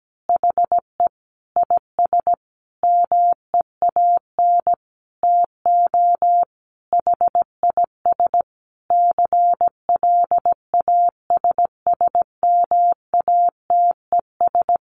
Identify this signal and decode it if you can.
Morse code, heard as a keyed tone, about 17 wpm, 700 Hz. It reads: HE IS MEAN TO HIS CLASSMATES
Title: Additional Easy Phrases 17wpm